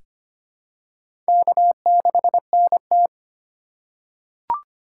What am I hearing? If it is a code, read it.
Morse code, heard as a keyed tone, about 25 wpm, 700 Hz. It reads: K6NT
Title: USA Call Signs - 4 Characters - 2x Character Spacing - 25wpm